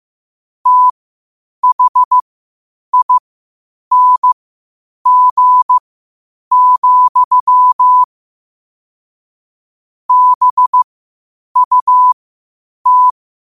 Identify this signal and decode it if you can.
Morse code, heard as a keyed tone, about 15 wpm, spaced as for 8 wpm, 1000 Hz. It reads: THING, BUT